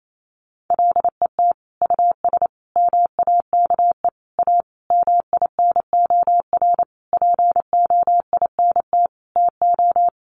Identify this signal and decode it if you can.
Morse code, heard as a keyed tone, about 28 wpm, 700 Hz. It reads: LET US MAKE A MINOR POINT TO
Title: Sentences from Top 1500 Words 28wpm